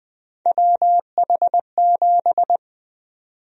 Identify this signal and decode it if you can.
Morse code, heard as a keyed tone, about 20 wpm, 700 Hz. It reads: WH7
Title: USA Call Sign Prefixes - Repeated 2x - 20wpm